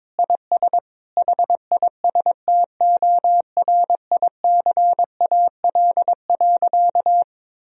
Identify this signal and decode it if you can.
Morse code, heard as a keyed tone, about 22 wpm, 700 Hz. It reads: IS HISTORICAL.
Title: Sentences from Top 1000 Words - Review Part 4 - 22wpm